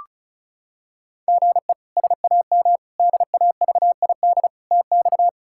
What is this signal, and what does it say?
Morse code, heard as a keyed tone, about 34 wpm, 700 Hz. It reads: GE SAM DAVID TX
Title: K1USN SST QSO Practice - 35wpm